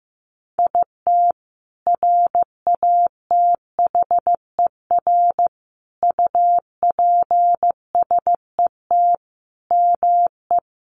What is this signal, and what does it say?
Morse code, heard as a keyed tone, about 15 wpm, 700 Hz. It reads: IT RATHER UPSET ME
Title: Sentences from Top 900 Words 15wpm